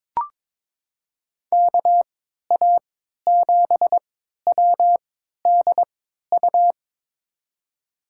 Morse code, heard as a keyed tone, about 22 wpm, spaced as for 13 wpm, 700 Hz. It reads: KA7WDU
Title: USA Call Signs - 6 Characters - 3x Character Spacing - 22wpm